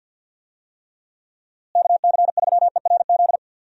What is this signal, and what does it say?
Morse code, heard as a keyed tone, about 48 wpm, 700 Hz. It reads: KC3ERZ
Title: USA Call Signs - 6 Characters - Repeated 3x - 50wpm